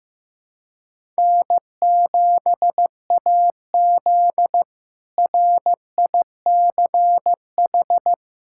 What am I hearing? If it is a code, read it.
Morse code, heard as a keyed tone, about 15 wpm, 700 Hz. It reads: N7AZ RICH
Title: K1USN SST QSO Practice - 15wpm